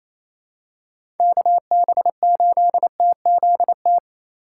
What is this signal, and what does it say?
Morse code, heard as a keyed tone, about 28 wpm, 700 Hz. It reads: KB8TZT
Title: USA Call Signs - 6 Characters - 2x Character Spacing - 28wpm